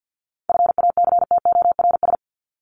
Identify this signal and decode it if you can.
Morse code, heard as a keyed tone, about 50 wpm, 700 Hz. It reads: FACTORS
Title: ES:S Suffix - Top 100 Words with Base Words 50wpm